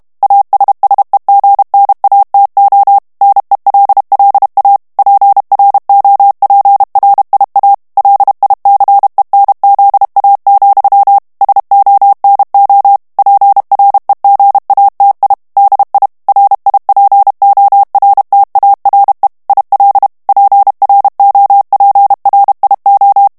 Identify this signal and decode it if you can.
Morse code, heard as a keyed tone, about 32 wpm, 800 Hz. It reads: ASSEGNATO NELLA PROPRIA LICENZA, SONO PREGATI DI RIPORTARE IL PROPRIO
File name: ITALIEN texte clair 2 vitesse 30 mots minute